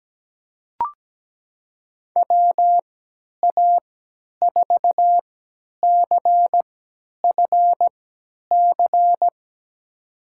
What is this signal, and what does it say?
Morse code, heard as a keyed tone, about 17 wpm, spaced as for 10 wpm, 700 Hz. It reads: WA4CFC
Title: USA Call Signs - 6 Characters - 3x Character Spacing - 17wpm